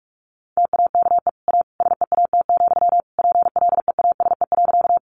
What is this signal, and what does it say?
Morse code, heard as a keyed tone, about 44 wpm, 700 Hz. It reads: TAKE A SEAT, PLEASE.
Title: Sentences from Top 1500 Words 45wpm